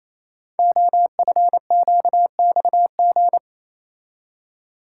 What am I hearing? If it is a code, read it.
Morse code, heard as a keyed tone, about 27 wpm, 700 Hz. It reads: OFQXG
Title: Code Groups - 5 Letters - 28wpm